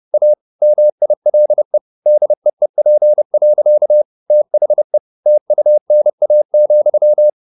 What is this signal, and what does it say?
Morse code, heard as a keyed tone, about 30 wpm, 600 Hz. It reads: A MILE DEEP. THE TUNA,
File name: the old man and the sea / the old man and the sea 1 30wpm